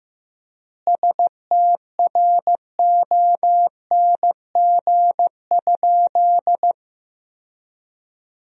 Morse code, heard as a keyed tone, about 16 wpm, 700 Hz. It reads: STRONG?
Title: 4x Word-Spacing - Sentences from Top 500 Words - 15wpm